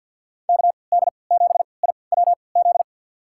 Morse code, heard as a keyed tone, about 50 wpm, spaced as for 28 wpm, 700 Hz. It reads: KD7IWZ
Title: USA Call Signs - 6 Characters - 3x Character Spacing - 50wpm